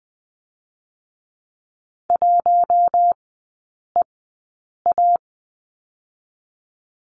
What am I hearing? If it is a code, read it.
Morse code, heard as a keyed tone, about 20 wpm, spaced as for 8 wpm, 700 Hz. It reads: 1EA
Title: 121 Lesson 20e - Number 3 - Three Characters 20wpm